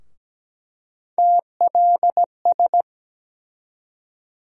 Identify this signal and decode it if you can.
Morse code, heard as a keyed tone, about 17 wpm, 700 Hz. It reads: TLS
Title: USA Call Sign Postfixes - 3 Letters 17wpm